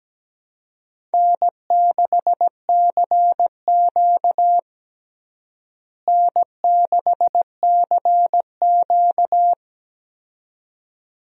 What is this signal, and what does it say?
Morse code, heard as a keyed tone, about 17 wpm, 700 Hz. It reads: N6CQ N6CQ
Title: USA Call Signs - 4 Characters - Repeated 2x - 17wpm